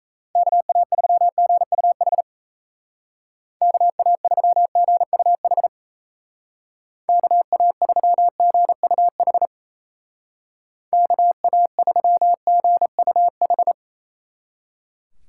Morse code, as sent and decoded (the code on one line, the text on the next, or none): -.- .- ...-- --. ..- .... / -.- .- ...-- --. ..- .... / -.- .- ...-- --. ..- .... / -.- .- ...-- --. ..- ....
KA3GUH KA3GUH KA3GUH KA3GUH